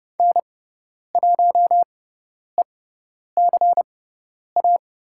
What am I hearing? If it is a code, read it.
Morse code, heard as a keyed tone, about 30 wpm, spaced as for 10 wpm, 700 Hz. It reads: N1ECA
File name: Lesson 17g - Letter u - Call Signs 30wpm